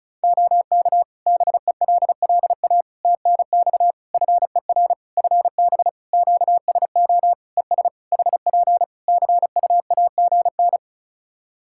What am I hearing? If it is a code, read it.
Morse code, heard as a keyed tone, about 34 wpm, 700 Hz. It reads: OK BELLA TNX FER FB QSO ES HP CUAGN
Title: K1USN SST QSO Practice 35wpm